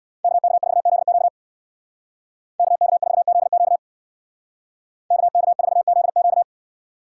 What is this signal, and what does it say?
Morse code, heard as a keyed tone, about 62 wpm, 700 Hz. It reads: KK4ZQ KK4ZQ KK4ZQ
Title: Speed Racing - USA Call Signs 50wpm